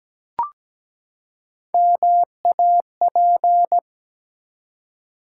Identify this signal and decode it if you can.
Morse code, heard as a keyed tone, about 17 wpm, 700 Hz. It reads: MAP
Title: Three Letter Words - 17wpm